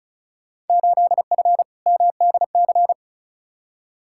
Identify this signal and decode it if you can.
Morse code, heard as a keyed tone, about 36 wpm, 700 Hz. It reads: 8F MDC
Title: Field Day Exchanges Encore - 35wpm